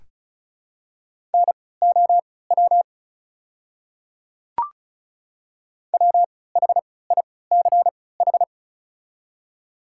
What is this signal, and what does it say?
Morse code, heard as a keyed tone, about 36 wpm, spaced as for 20 wpm, 690 Hz. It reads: NOW WHICH
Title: 3x Character Spacing - Top 100 Words - 35wpm